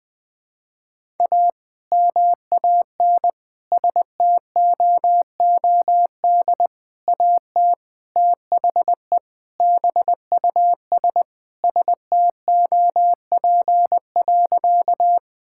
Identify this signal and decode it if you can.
Morse code, heard as a keyed tone, about 20 wpm, 700 Hz. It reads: A MAN STOOD AT THE BUS STOP.
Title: Sentences with 2-Word Phrases - 20wpm